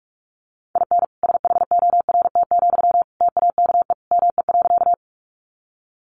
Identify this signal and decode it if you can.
Morse code, heard as a keyed tone, about 48 wpm, 700 Hz. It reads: IN SHORT, TAKE ME.
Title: Sentences from Top 400 Words 45wpm